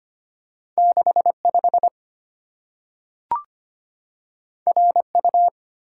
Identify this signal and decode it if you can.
Morse code, heard as a keyed tone, about 25 wpm, 690 Hz. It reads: 65 RU
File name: Two Letter-Number Combinations - 25wpm